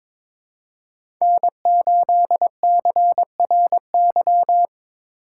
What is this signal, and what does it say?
Morse code, heard as a keyed tone, about 22 wpm, 700 Hz. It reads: N8CRY
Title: USA Call Signs - 5 Characters - 2x Character Spacing - 22wpm